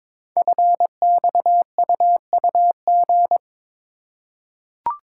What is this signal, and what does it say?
Morse code, heard as a keyed tone, about 22 wpm, 700 Hz. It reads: FXUUG
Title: Code Groups - 5 Letters - 22wpm